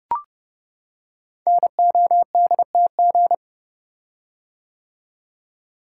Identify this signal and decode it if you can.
Morse code, heard as a keyed tone, about 30 wpm, 700 Hz. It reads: NODTG
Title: Code Groups - 5 Letters - 30wpm